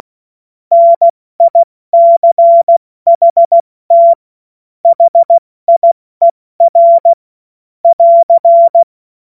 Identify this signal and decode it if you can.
Morse code, heard as a keyed tone, about 16 wpm, 680 Hz. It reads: NICHT HIER <AR>
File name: Morsekurs / Witze / Witz 30. Dezember 2025 70 F80